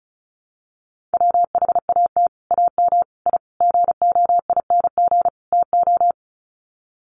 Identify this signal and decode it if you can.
Morse code, heard as a keyed tone, about 34 wpm, 700 Hz. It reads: WHAT AM I GOING TO
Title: Sets of 5 Words Encore 35wpm